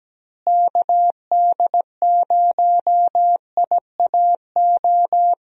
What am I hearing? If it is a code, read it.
Morse code, heard as a keyed tone, about 17 wpm, 700 Hz. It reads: KD0IAO
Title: USA Call Signs - 6 Characters - Repeated 3x - 17wpm